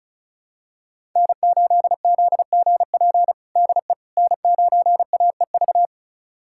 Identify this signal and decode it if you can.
Morse code, heard as a keyed tone, about 34 wpm, 690 Hz. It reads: N8ZGP DE N9AEV
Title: USA Callsign DE Callsign - 5 Characters - 35wpm